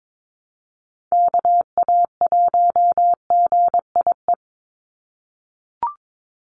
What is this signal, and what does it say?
Morse code, heard as a keyed tone, about 22 wpm, 700 Hz. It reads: KA1GIE
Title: USA Call Signs 22wpm